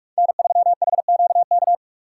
Morse code, heard as a keyed tone, about 45 wpm, 700 Hz. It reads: N3HQK